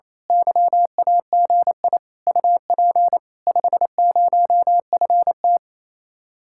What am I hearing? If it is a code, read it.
Morse code, heard as a keyed tone, about 27 wpm, 700 Hz. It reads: YAGI UP 50FT
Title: Speed-Racing Common QSO Phrases - 28wpm